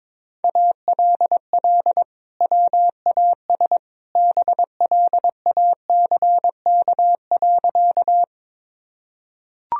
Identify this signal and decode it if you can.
Morse code, heard as a keyed tone, about 22 wpm, 700 Hz. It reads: ALL WAS BLACK.
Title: Sentences from Top 1000 Words - Review Part 2 - 22wpm